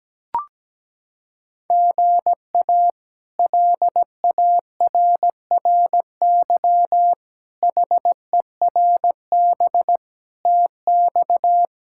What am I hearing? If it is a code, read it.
Morse code, heard as a keyed tone, about 17 wpm, 700 Hz. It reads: GA LARRY HERB TX
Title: K1USN SST QSO Practice - 17wpm